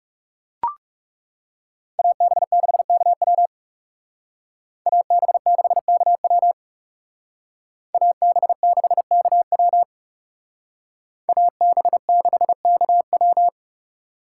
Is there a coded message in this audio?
Yes: Morse code from 2.0 to 13.5 s, heard as a keyed tone, about 39 wpm, 700 Hz. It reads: AB6KW AB6KW AB6KW AB6KW